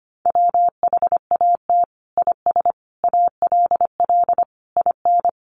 Morse code, heard as a keyed tone, about 25 wpm, 700 Hz. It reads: WHAT IS ALL IN
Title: Sentences from Top 1200 Words 25wpm